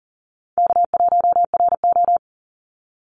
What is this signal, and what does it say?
Morse code, heard as a keyed tone, about 40 wpm, 700 Hz. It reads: K1RO
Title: Easy Call Signs Repeated 2x - 40wpm